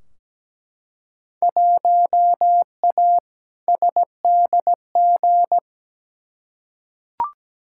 Morse code, heard as a keyed tone, about 17 wpm, 700 Hz. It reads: 1A SDG
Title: Field Day Exchanges Encore - 17wpm